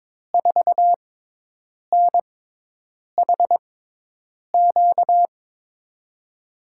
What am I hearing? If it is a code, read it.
Morse code, heard as a keyed tone, about 22 wpm, spaced as for 8 wpm, 700 Hz. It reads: 4NHQ
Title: USA Call Signs - 5 Characters - 6x Character Spacing - 22wpm